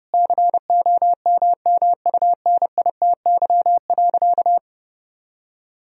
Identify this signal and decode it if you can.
Morse code, heard as a keyed tone, about 30 wpm, 700 Hz. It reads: COMMUNITY.
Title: Sentences from Top 1000 Words - Review Part 4 - 30wpm